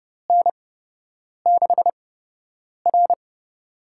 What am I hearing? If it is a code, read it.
Morse code, heard as a keyed tone, about 30 wpm, spaced as for 8 wpm, 700 Hz. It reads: N6R
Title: USA Call Signs - 4 Characters - 8x Character Spacing - 30wpm